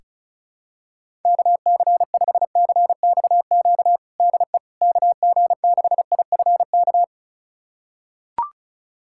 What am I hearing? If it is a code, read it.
Morse code, heard as a keyed tone, about 36 wpm, 700 Hz. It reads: KC5CXQ DE KG6IFK
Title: USA Callsign DE Callsign 35wpm